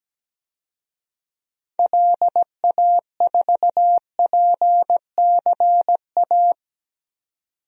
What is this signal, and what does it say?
Morse code, heard as a keyed tone, about 17 wpm, 700 Hz. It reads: LA4PCA
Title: DXCC-100 Call Signs - 6 Characters - 17wpm